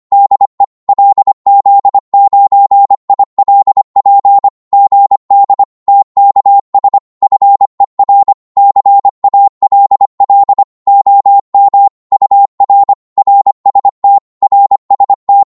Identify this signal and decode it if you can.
Morse code, heard as a keyed tone, about 25 wpm, 800 Hz. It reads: DE LZ9ILP GD TKS FER CALL OM UR RST RST
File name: Morse / qso-25wpm